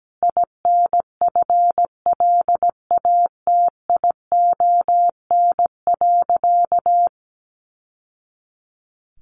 Morse code, heard as a keyed tone, about 17 wpm, 700 Hz. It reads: INFLATION.
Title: Sentences from Top 1100 Words 17wpm